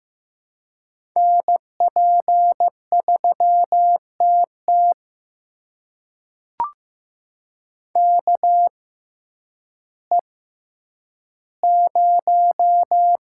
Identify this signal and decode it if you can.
Morse code, heard as a keyed tone, about 15 wpm, spaced as for 9 wpm, 700 Hz. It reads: NP3TT KE0
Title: USA Call Signs - 6x Character Spacing - 15wpm